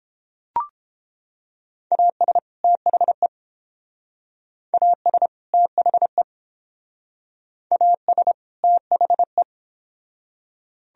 Morse code, heard as a keyed tone, about 29 wpm, 700 Hz. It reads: AS THE AS THE AS THE